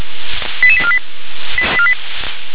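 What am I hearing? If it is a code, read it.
Five-tone selective call (ZVEI), heard as quick runs of five tones, tones of about 70 ms. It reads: 88848
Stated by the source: ZVEI five-tone selective call 88702 received by SDR.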